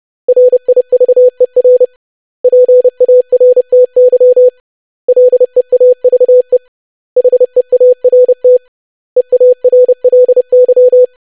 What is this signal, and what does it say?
Morse code, heard as a keyed tone, about 30 wpm, 500 Hz. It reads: RIVER PARTY LEAVE HEART EARLY
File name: lesson / web / basic500-25wpm